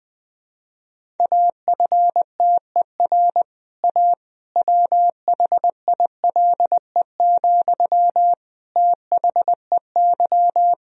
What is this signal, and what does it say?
Morse code, heard as a keyed tone, about 20 wpm, 700 Hz. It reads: AFTER A WHILE, THEY
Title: Sentences from Top 1000 Words - Review Part 1 - 20wpm